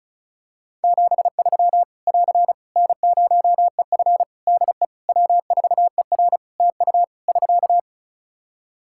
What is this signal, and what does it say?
Morse code, heard as a keyed tone, about 34 wpm, 700 Hz. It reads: 73 <AR> N0EF DE W4ER TU <SK>
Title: Practice QSO - Exchange 4 - Ending - 35wpm